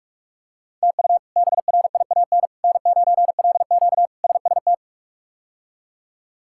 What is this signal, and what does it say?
Morse code, heard as a keyed tone, about 46 wpm, 700 Hz. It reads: TU BRIAN N9LQ SST
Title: K1USN SST QSO Practice - 45wpm